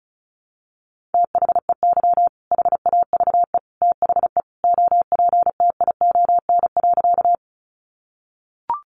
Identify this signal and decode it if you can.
Morse code, heard as a keyed tone, about 34 wpm, 700 Hz. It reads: THEY HAVE THE OPTION.
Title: Sentences from Top 1300 Words 35wpm